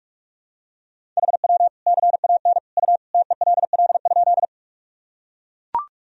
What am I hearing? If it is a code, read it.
Morse code, heard as a keyed tone, about 44 wpm, 700 Hz. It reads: HW CAN U TELL?
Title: Sentences from Top 200 Words and Abbreviations 45wpm